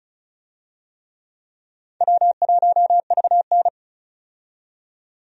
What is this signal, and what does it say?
Morse code, heard as a keyed tone, about 36 wpm, 700 Hz. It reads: W1VN
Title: USA Call Signs - 4 Characters - 35wpm